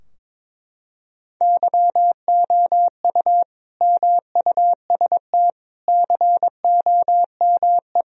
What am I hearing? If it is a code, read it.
Morse code, heard as a keyed tone, about 22 wpm, 700 Hz. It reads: YOU MUST COME
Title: Sentences from Top 1000 Words - Review Part 2 - 22wpm